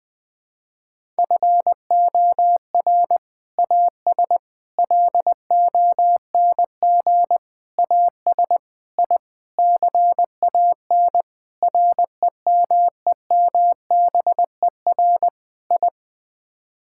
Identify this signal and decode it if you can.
Morse code, heard as a keyed tone, about 20 wpm, 700 Hz. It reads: FOR AS LONG AS I CAN REMEMBER I
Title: Sets of 8 Words - 20wpm